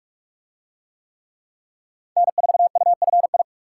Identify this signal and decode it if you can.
Morse code, heard as a keyed tone, about 40 wpm, 700 Hz. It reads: N4UFI